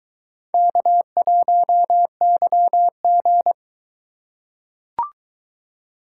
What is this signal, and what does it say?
Morse code, heard as a keyed tone, about 22 wpm, 700 Hz. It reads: K1YG